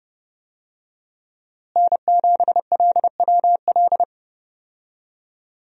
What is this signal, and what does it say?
Morse code, heard as a keyed tone, about 30 wpm, 700 Hz. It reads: N7LWL
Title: USA Call Signs - 5 Characters - Repeated 3x - 30wpm